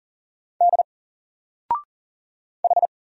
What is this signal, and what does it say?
Morse code, heard as a keyed tone, about 41 wpm, 700 Hz. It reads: D H
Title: Single Character (Letter:Number:Punctuation:Prosign) 40wpm